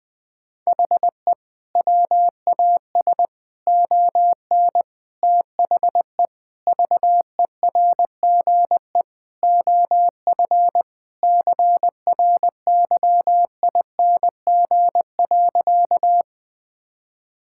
Morse code, heard as a keyed tone, about 20 wpm, 700 Hz. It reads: HE WAS ON THE VERGE OF CRYING.
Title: Sentences with 4-Word Phrases - 20wpm